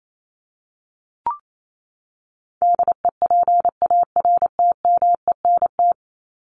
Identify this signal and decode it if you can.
Morse code, heard as a keyed tone, about 28 wpm, 700 Hz. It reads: DEPARTMENT
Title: Top 751-1000 Words 28wpm